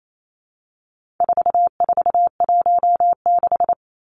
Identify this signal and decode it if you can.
Morse code, heard as a keyed tone, about 28 wpm, 700 Hz. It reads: 4416
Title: Numbers - 4-Digit 28wpm